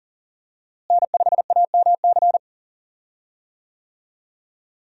Morse code, heard as a keyed tone, about 37 wpm, 700 Hz. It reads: N5AMC
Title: USA Call Signs - 5 Characters - Repeated 2x - 40wpm